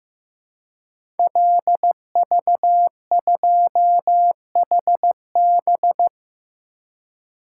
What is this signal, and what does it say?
Morse code, heard as a keyed tone, about 15 wpm, 700 Hz. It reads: LV2HB
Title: Code Groups - 5 Letter:Numbers - 15wpm